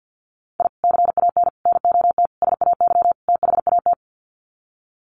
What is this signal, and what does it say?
Morse code, heard as a keyed tone, about 50 wpm, 700 Hz. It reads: I CAN NOT SAY THAT
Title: Sets of 5 Words Encore2 50wpm